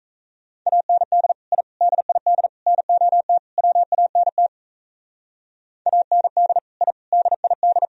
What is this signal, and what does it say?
Morse code, heard as a keyed tone, about 35 wpm, 700 Hz. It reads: AND I DID NOT WANT AND I DID